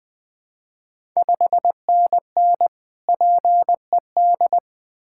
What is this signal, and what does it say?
Morse code, heard as a keyed tone, about 20 wpm, 690 Hz. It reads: 5NN PED
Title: Ontario QSO Party Exchange 20wpm